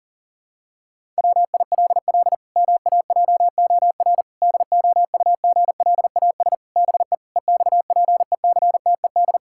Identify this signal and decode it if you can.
Morse code, heard as a keyed tone, about 37 wpm, 700 Hz. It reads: WILL MAJOR DOUGLAS BE EXPECTED
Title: Pangram Challenge v2 - 40wpm